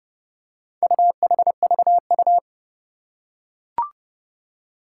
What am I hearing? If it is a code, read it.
Morse code, heard as a keyed tone, about 30 wpm, 700 Hz. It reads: UHVU
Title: Four Challenging Letter-Number Combinations 30wpm